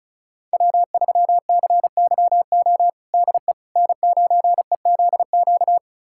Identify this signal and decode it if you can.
Morse code, heard as a keyed tone, about 36 wpm, 700 Hz. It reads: W3CYO DE N9EZQ
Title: USA Callsign DE Callsign - 5 Characters - 35wpm